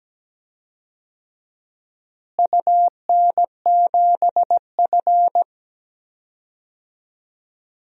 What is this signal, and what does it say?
Morse code, heard as a keyed tone, about 17 wpm, 700 Hz. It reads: UN7F
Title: DXCC-100 Call Signs - 4 Characters - Repeated 2x - 17wpm